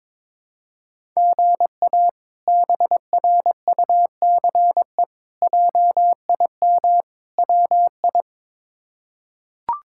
Morse code, heard as a keyed tone, about 21 wpm, 700 Hz. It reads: GA BRUCE JIM WI
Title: K1USN SST QSO Practice - 22wpm